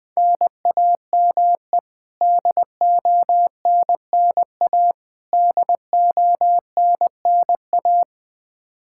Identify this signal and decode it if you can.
Morse code, heard as a keyed tone, about 20 wpm, 700 Hz. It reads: NAME DONNA DONNA
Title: K1USN SST QSO Practice 20wpm